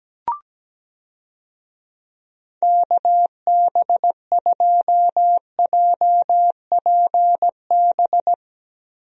Morse code, heard as a keyed tone, about 17 wpm, 700 Hz. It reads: KB2JPB
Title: USA Call Signs - 6 Characters - Repeated 3x - 17wpm